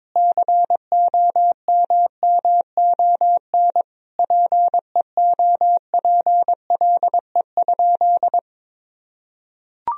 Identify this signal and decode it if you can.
Morse code, heard as a keyed tone, about 22 wpm, 700 Hz. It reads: COMMON PEOPLE?
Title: Sentences from Top 1000 Words - Review Part 3 - 22wpm